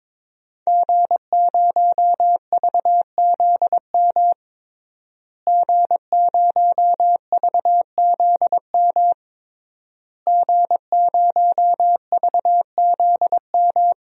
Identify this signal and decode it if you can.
Morse code, heard as a keyed tone, about 22 wpm, 700 Hz. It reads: G0VZM G0VZM G0VZM
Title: DXCC-100 Call Signs - 5 Characters - Repeated 3x - 22wpm